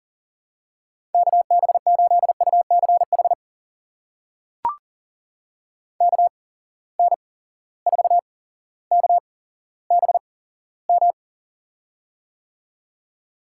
Morse code, heard as a keyed tone, about 40 wpm, spaced as for 17 wpm, 700 Hz. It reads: KB8UCH KN4KBM
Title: USA Call Signs - 6 Characters - 8x Character Spacing - 40wpm